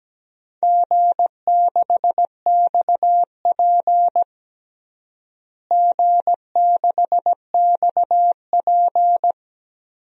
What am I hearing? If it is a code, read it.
Morse code, heard as a keyed tone, about 17 wpm, 700 Hz. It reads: G6XP G6XP
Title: DXCC-100 Call Signs - 4 Characters - Repeated 3x - 17wpm